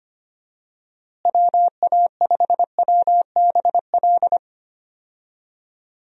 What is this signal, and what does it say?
Morse code, heard as a keyed tone, about 25 wpm, 700 Hz. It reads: WA5WBL
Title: USA Call Signs - 6 Characters - Repeated 3x - 25wpm